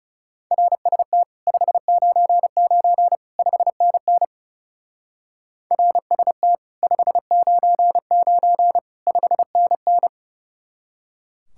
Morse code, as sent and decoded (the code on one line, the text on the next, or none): .-. ... - / ..... ----. ----. / ..... -. -. / .-. ... - / ..... ----. ----. / ..... -. -.
RST 599 5NN RST 599 5NN